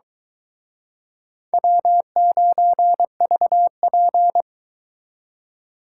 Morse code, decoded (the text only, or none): W9VP